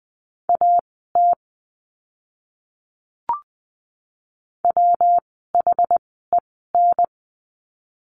Morse code, heard as a keyed tone, about 20 wpm, spaced as for 14 wpm, 700 Hz. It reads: AT WHEN
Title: 139 Lesson 23f - Letter f - Words 20wpm